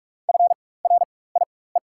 Morse code, heard as a keyed tone, about 43 wpm, spaced as for 21 wpm, 690 Hz. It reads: FRIE
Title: 4x Character Spacing - Top 500 Words - 45wpm